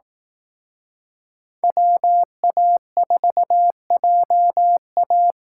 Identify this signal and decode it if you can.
Morse code, heard as a keyed tone, about 18 wpm, 700 Hz. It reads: WA4JA
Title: Speed Racing - USA Call Signs 15wpm